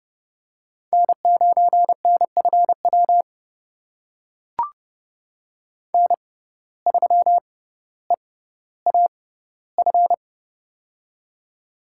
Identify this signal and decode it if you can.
Morse code, heard as a keyed tone, about 30 wpm, spaced as for 15 wpm, 700 Hz. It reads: N9NFW N3EAF
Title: USA Call Signs - 5 Characters - 6x Character Spacing - 30wpm